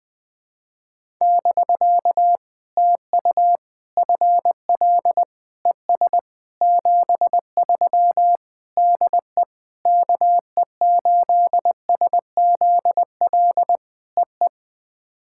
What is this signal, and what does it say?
Morse code, heard as a keyed tone, about 21 wpm, 700 Hz. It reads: <BK> TU FL ES 73 DE KE8SZL EE
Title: POTA QSO Practice - 20wpm